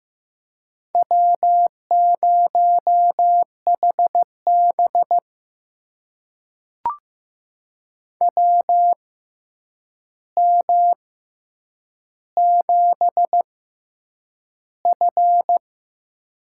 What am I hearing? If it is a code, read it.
Morse code, heard as a keyed tone, about 15 wpm, spaced as for 8 wpm, 700 Hz. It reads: W0HB WM7F
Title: USA Call Signs - 4 Characters - 6x Character Spacing - 15wpm